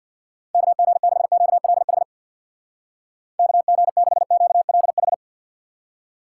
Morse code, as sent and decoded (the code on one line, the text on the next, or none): -.- --. -.... --.- .-.. .... / -.- --. -.... --.- .-.. ....
KG6QLH KG6QLH